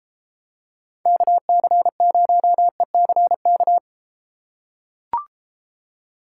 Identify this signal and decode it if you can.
Morse code, heard as a keyed tone, about 33 wpm, 700 Hz. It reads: KC0ECK